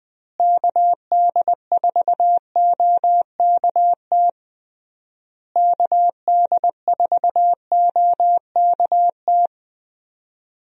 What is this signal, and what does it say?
Morse code, heard as a keyed tone, about 20 wpm, 700 Hz. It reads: KD4OKT KD4OKT
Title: USA Call Signs - 6 Characters - Repeated 3x - 20wpm